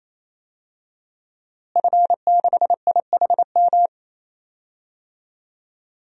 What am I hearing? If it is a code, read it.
Morse code, heard as a keyed tone, about 28 wpm, 700 Hz. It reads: F6IHM
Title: DXCC-100 Call Signs - 5 Characters - Repeated 3x - 28wpm